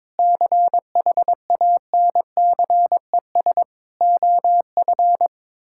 Morse code, heard as a keyed tone, about 22 wpm, 700 Hz. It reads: CHANCES OF
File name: practice / 22wpm / Sentences with 8-Word Phrases - 22wpm